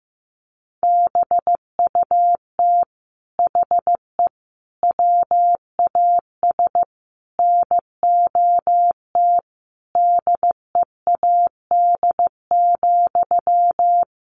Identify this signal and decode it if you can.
Morse code, heard as a keyed tone, about 15 wpm, 700 Hz. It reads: BUT HE WAS NOT DEAD,
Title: Sentences from Top 1500 Words 15wpm